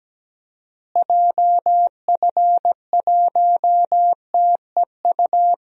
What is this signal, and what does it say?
Morse code, heard as a keyed tone, about 17 wpm, 700 Hz. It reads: JF1TEU
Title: DXCC-100 Call Signs - 6 Characters - Repeated 2x - 17wpm